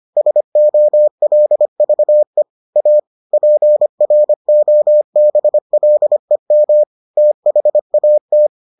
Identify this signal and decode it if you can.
Morse code, heard as a keyed tone, about 25 wpm, 600 Hz. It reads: SOLVE A PROBLEM THAT